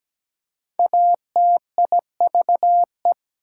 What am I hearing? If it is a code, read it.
Morse code, heard as a keyed tone, about 17 wpm, 700 Hz. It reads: ATIVE
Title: Prefixes and Suffixes 17wpm